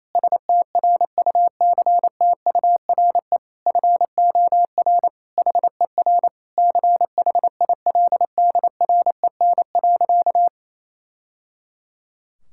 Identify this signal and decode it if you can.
Morse code, heard as a keyed tone, about 27 wpm, 700 Hz. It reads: STRUCTURE FOR HER CHILDREN.
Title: Sentences from Top 1000 Words - Review Part 6 - 28wpm